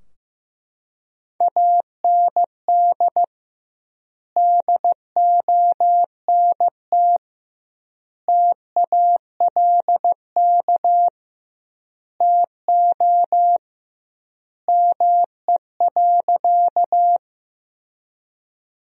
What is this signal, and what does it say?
Morse code, heard as a keyed tone, about 15 wpm, 700 Hz. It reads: AND DONT TALK TO ME.
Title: 2x Word-Spacing - Sentences from Top 400 Words - 15wpm